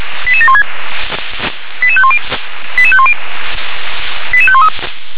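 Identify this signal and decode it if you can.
Five-tone selective call (ZVEI), heard as quick runs of five tones, tones of about 80 ms. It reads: 88716 88510 88510 88513
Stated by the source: ZVEI five-tone selective call 88758 received by SDR.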